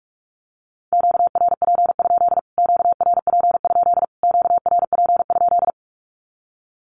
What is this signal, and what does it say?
Morse code, heard as a keyed tone, about 44 wpm, 700 Hz. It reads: QRP? QRP? QRP?
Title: Q-Codes Repeated 3x 45wpm